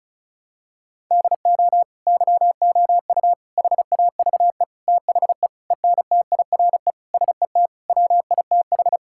Morse code, heard as a keyed tone, about 34 wpm, 700 Hz. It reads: DO YOU HAVE THE ENTIRE SET WITH
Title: Sentences from Top 1000 Words - Review Part 5 - 35wpm